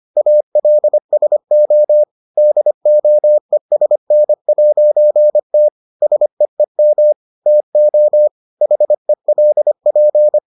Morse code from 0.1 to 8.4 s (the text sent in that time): ALSO DOESN'T SEEM TO